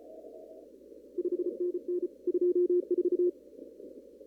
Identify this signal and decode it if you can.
Morse code, heard as a keyed tone, about 34 wpm, 360 Hz. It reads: 5NN 24
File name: CW / kt5j wpx cw-op-k5pi